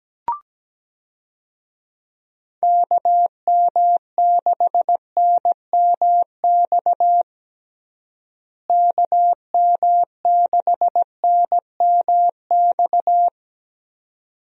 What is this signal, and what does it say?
Morse code, heard as a keyed tone, about 17 wpm, 700 Hz. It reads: KM6NMX KM6NMX
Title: USA Call Signs - 6 Characters - Repeated 2x - 17wpm